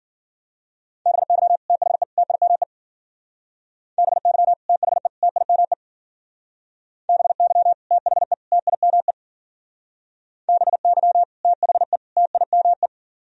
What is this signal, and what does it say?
Morse code, heard as a keyed tone, about 52 wpm, 700 Hz. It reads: BY THE TIME BY THE TIME BY THE TIME BY THE TIME